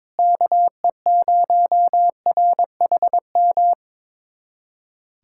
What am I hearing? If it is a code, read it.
Morse code, heard as a keyed tone, about 22 wpm, 700 Hz. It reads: KE0RHM
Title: USA Callsign DE Callsign - 6 Characters - 22wpm